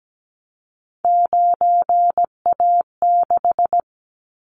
Morse code, heard as a keyed tone, about 17 wpm, 700 Hz. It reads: 9A6
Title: Callsign Prefixes for Top 25 DXCC - 17wpm